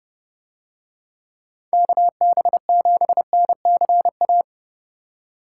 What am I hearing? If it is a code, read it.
Morse code, heard as a keyed tone, about 29 wpm, 700 Hz. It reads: KB7NCA